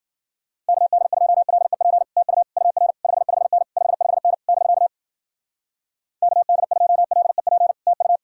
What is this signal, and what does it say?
Morse code, heard as a keyed tone, about 49 wpm, 700 Hz. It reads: KD2LEP TU UR 55N 55N <BK> KD2LEP TU
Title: Speed-Racing Common QSO Phrases - 40wpm